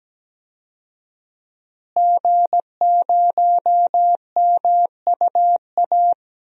Morse code, heard as a keyed tone, about 17 wpm, 700 Hz. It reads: G0MUA
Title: DXCC-100 Call Signs - 5 Characters - Repeated 3x - 17wpm